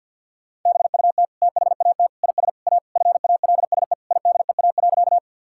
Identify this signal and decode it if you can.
Morse code, heard as a keyed tone, about 50 wpm, 700 Hz. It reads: BUT THAT IS A FALSE IDEA.
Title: Sentences from Top 1600 Words 50wpm